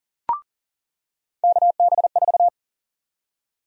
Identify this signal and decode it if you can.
Morse code, heard as a keyed tone, about 40 wpm, 700 Hz. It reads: KB4
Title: USA Call Sign Prefixes - 40wpm